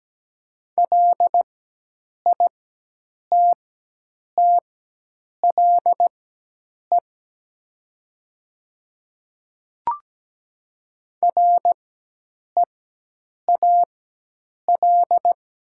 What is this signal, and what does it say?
Morse code, heard as a keyed tone, about 17 wpm, spaced as for 8 wpm, 700 Hz. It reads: LITTLE REAL
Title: 4x Character Spacing - Top 300 Words - 17wpm